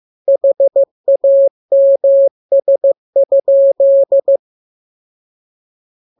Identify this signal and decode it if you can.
Morse code, heard as a keyed tone, about 15 wpm, 550 Hz. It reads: HAMS?
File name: jokes / joke 13 15 0000